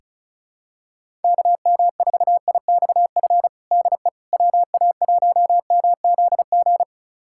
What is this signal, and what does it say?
Morse code, heard as a keyed tone, about 36 wpm, 700 Hz. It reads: KM4IXF DE WA1MZG
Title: USA Callsign DE Callsign 35wpm